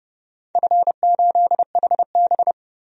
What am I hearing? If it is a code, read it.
Morse code, heard as a keyed tone, about 32 wpm, 700 Hz. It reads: F8HB
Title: DXCC-100 Call Signs - 4 Characters - 30wpm